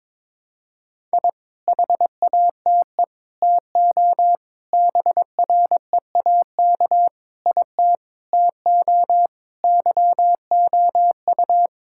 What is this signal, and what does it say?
Morse code, heard as a keyed tone, about 21 wpm, 700 Hz. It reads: I HATE TO BREAK IT TO YOU
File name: practice / 22wpm / Sets of 8 Words - 22wpm